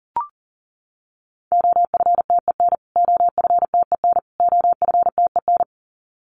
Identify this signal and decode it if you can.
Morse code, heard as a keyed tone, about 40 wpm, 700 Hz. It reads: OFTEN OFTEN OFTEN
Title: Top 300 Words Repeated 40wpm